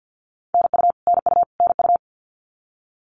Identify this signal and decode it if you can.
Morse code, heard as a keyed tone, about 50 wpm, 700 Hz. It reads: NU NU NU
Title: Canadian Provinces and Territories - Repeated 3x - 50wpm